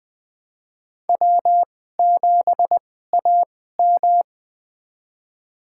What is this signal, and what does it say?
Morse code, heard as a keyed tone, about 20 wpm, spaced as for 14 wpm, 700 Hz. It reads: W7AM
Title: USA Call Signs - 4 Characters - 2x Character Spacing - 20wpm